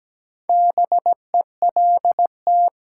Morse code, heard as a keyed tone, about 17 wpm, 700 Hz. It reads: BELT
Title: Sentences from Top 1400 Words - New Word Drill - 17wpm